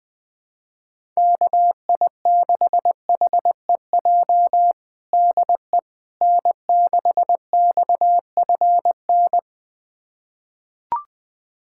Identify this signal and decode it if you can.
Morse code, heard as a keyed tone, about 21 wpm, 700 Hz. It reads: KI6HEJ DE N6XFN
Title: USA Callsign DE Callsign 20wpm